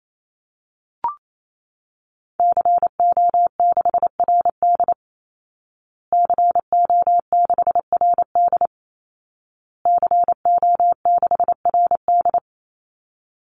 Morse code, heard as a keyed tone, about 28 wpm, 700 Hz. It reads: CO6RD CO6RD CO6RD
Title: Call Signs Repeated 2x - 28wpm